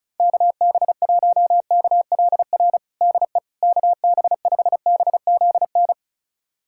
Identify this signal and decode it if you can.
Morse code, heard as a keyed tone, about 34 wpm, 700 Hz. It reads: KB1KLR DE KB5BZN
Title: USA Callsign DE Callsign - 6 Characters - 35wpm